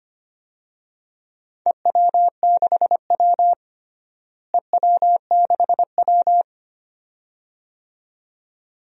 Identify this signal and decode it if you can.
Morse code, heard as a keyed tone, about 25 wpm, 700 Hz. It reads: EW6W EW6W
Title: DXCC-100 Call Signs - 4 Characters - Repeated 2x - 25wpm